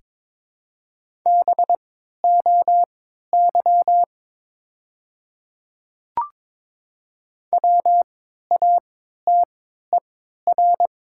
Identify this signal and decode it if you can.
Morse code, heard as a keyed tone, about 23 wpm, spaced as for 13 wpm, 700 Hz. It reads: BOY WATER
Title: 3x Character Spacing - Top 200 Words - 22wpm